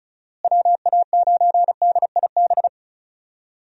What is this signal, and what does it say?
Morse code, heard as a keyed tone, about 34 wpm, 700 Hz. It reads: WA9DIB
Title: USA Call Signs - 6 Characters - Repeated 2x - 35wpm